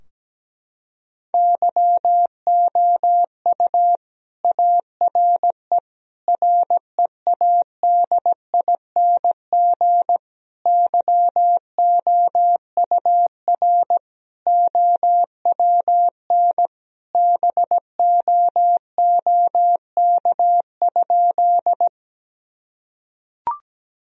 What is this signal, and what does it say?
Morse code, heard as a keyed tone, about 17 wpm, 700 Hz. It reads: YOU ARE READING YOUR OWN BOOK?
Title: Sentences from Top 1000 Words - Review Part 3 - 17wpm